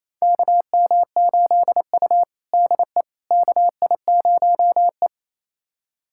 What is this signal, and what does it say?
Morse code, heard as a keyed tone, about 28 wpm, 700 Hz. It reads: KM8U DE KI0E
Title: USA Callsign DE Callsign - 4 Characters - 28wpm